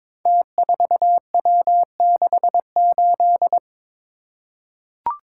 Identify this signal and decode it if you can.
Morse code, heard as a keyed tone, about 22 wpm, 700 Hz. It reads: T4W68
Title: Code Groups - 5 Letter:Numbers - 22wpm